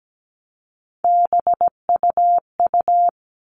Morse code, heard as a keyed tone, about 17 wpm, 700 Hz. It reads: BUU
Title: All Cut Numbers - 3-Digit 17wpm